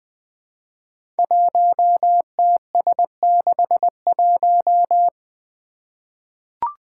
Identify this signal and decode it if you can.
Morse code, heard as a keyed tone, about 20 wpm, 700 Hz. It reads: 1TS61
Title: Five Challenging Letter-Number Combinations 20wpm